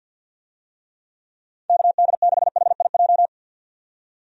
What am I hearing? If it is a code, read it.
Morse code, heard as a keyed tone, about 51 wpm, 700 Hz. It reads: KD6HIJ
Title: USA Call Signs - 6 Characters - 50wpm